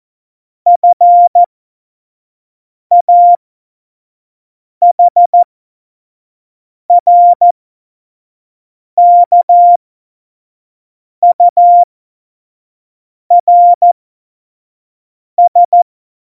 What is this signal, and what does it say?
Morse code, heard as a keyed tone, about 14 wpm, spaced as for 5 wpm, 700 Hz. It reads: FAHRKURS
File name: Morsekurs / Witze / Witz 04 August 2020, 25 F70